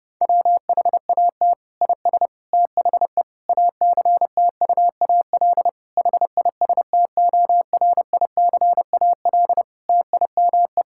Morse code, heard as a keyed tone, about 30 wpm, 700 Hz. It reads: WHAT IS THE ACTUAL HISTORICAL TIME
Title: Sentences from Top 1000 Words - Review Part 5 - 30wpm